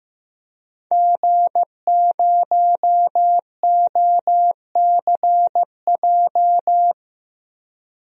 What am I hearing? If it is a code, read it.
Morse code, heard as a keyed tone, about 15 wpm, 700 Hz. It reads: G0OCJ
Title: DXCC-100 Call Signs - 5 Characters - 15wpm